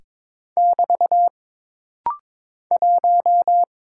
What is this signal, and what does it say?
Morse code, heard as a keyed tone, about 22 wpm, 700 Hz. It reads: <BT> 1
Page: This rapid-fire timing is geared to push your ICR (Instant Character Recognition) skills.